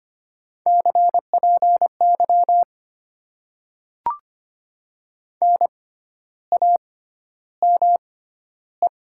Morse code, heard as a keyed tone, about 25 wpm, spaced as for 12 wpm, 700 Hz. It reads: CPY NAME
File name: Top 90 QSO Elements - 6x Character Spacing - 25wpm